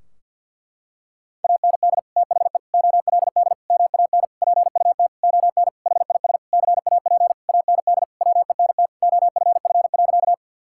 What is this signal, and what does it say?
Morse code, heard as a keyed tone, about 50 wpm, 700 Hz. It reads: AND THE OLD MAN PUT ON HIS CAP AND WENT OFF.
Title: Sentences from Top 1600 Words 50wpm